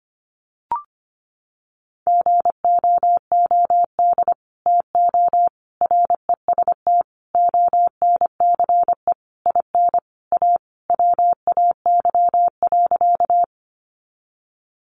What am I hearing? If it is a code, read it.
Morse code, heard as a keyed tone, about 25 wpm, 700 Hz. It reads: GOOD TO REST ONCE IN A WAY.
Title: Sentences from Top 500 Words 25wpm